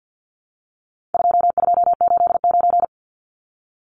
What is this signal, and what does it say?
Morse code, heard as a keyed tone, about 50 wpm, 700 Hz. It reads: 2289
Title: Numbers - 4-Digit 50wpm